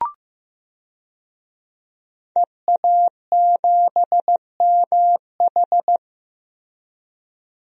Morse code, heard as a keyed tone, about 15 wpm, 700 Hz. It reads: EA7MH
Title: DXCC-100 Call Signs - 5 Characters - Repeated 2x - 15wpm